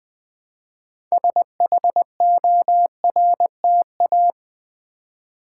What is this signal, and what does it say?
Morse code, heard as a keyed tone, about 20 wpm, 700 Hz. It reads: SHORTA
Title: Build A Word - 3 Letters At A Time - 8-Letter Words - 20wpm